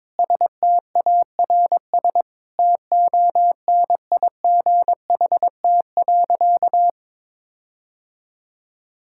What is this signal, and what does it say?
Morse code, heard as a keyed tone, about 22 wpm, 700 Hz. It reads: STARS TONIGHT.
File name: practice / 22wpm / Sentences with 8-Word Phrases - 22wpm